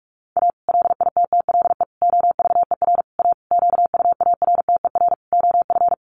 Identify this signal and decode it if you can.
Morse code, heard as a keyed tone, about 44 wpm, 700 Hz. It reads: A LITTLE OVER A QUARTER OF
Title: Sentences from Top 1300 Words 45wpm